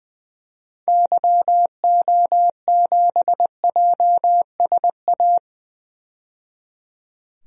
Morse code, heard as a keyed tone, about 21 wpm, 700 Hz. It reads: YO7JSA
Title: DXCC-100 Call Signs - 6 Characters - 20wpm